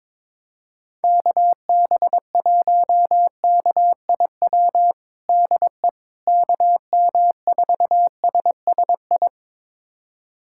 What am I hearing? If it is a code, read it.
Morse code, heard as a keyed tone, about 22 wpm, 700 Hz. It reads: KB1KIW DE KM4SSI
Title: USA Callsign DE Callsign 22wpm